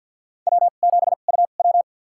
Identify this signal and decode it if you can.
Morse code, heard as a keyed tone, about 52 wpm, spaced as for 36 wpm, 690 Hz. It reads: W7UW
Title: USA Call Signs - 2x Character Spacing - 50wpm